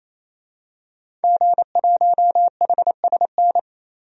Morse code, heard as a keyed tone, about 27 wpm, 700 Hz. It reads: G1HSN
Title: DXCC-100 Call Signs - 5 Characters - 28wpm